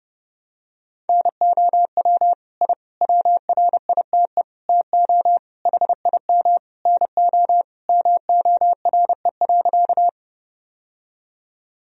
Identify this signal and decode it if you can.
Morse code, heard as a keyed tone, about 30 wpm, 700 Hz. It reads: NOW I WRITE TO HIM NO MORE.
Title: Sentences from Top 1000 Words - Review Part 1 - 30wpm